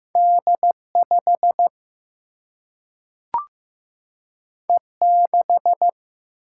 Morse code, heard as a keyed tone, about 15 wpm, 700 Hz. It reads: D5 E6
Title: Single-Letter Single-Number 15wpm